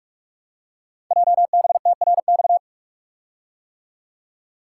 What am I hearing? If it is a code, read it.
Morse code, heard as a keyed tone, about 44 wpm, 700 Hz. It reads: JBTRX
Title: Code Groups - 5 Letters - 45wpm